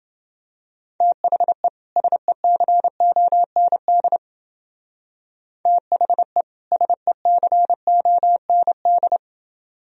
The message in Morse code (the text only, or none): THE SECOND THE SECOND